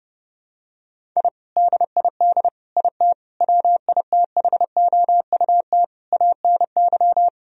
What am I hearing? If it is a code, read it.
Morse code, heard as a keyed tone, about 30 wpm, 700 Hz. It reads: I DID IT WITHOUT ANY
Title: Sentences from Top 1000 Words - Review Part 3 - 30wpm